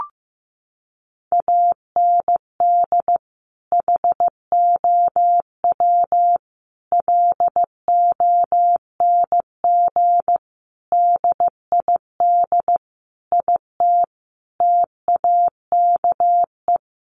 Morse code, heard as a keyed tone, about 15 wpm, 700 Hz. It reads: AND HOW LONG DID IT TAKE
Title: Sentences from Top 1500 Words 15wpm